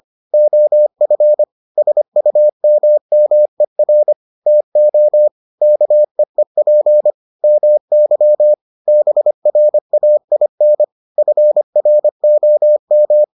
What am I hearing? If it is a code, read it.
Morse code, heard as a keyed tone, about 25 wpm, 600 Hz. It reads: OF SUMMER TO KEEP MY BRAIN FROM